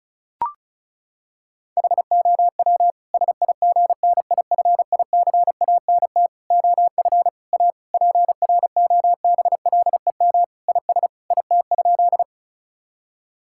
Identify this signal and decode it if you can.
Morse code, heard as a keyed tone, about 36 wpm, 700 Hz. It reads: HOW SIGNIFICANT OF A PROBLEM IS IT?
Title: Sentences from Top 1000 Words - Review Part 6 - 35wpm